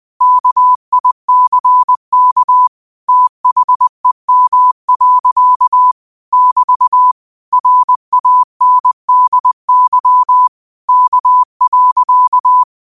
Morse code, heard as a keyed tone, about 20 wpm, 1000 Hz. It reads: KICK THEM. <BT> RANDY K.
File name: qotdfiles / 20wpm